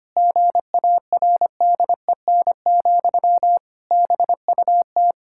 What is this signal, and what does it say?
Morse code, heard as a keyed tone, about 25 wpm, 700 Hz. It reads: GARDEN, BUT
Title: Pangram Challenge v2 - 25wpm